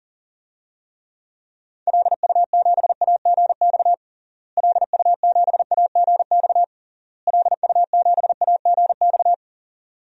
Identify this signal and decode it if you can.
Morse code, heard as a keyed tone, about 40 wpm, 700 Hz. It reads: LU7AGX LU7AGX LU7AGX
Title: DXCC-100 Call Signs - 6 Characters - Repeated 3x - 40wpm